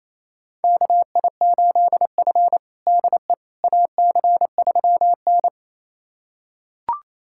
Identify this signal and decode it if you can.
Morse code, heard as a keyed tone, about 28 wpm, 700 Hz. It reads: KI8F DE AC3N
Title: USA Callsign DE Callsign - 4 Characters - 28wpm